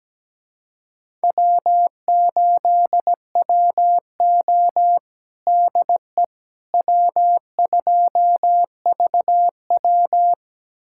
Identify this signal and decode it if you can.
Morse code, heard as a keyed tone, about 17 wpm, 700 Hz. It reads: W8WO DE W2VW
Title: USA Callsign DE Callsign - 4 Characters - 17wpm